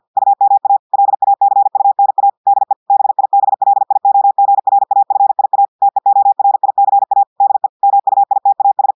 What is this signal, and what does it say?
Morse code, heard as a keyed tone, about 50 wpm, 800 Hz. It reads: UMA LACUNA DE BIBLIOGRAFIA TEORICA DE MUITAS
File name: audios / portugais / portugais texte en clair 1 vitesse 50 mots minutes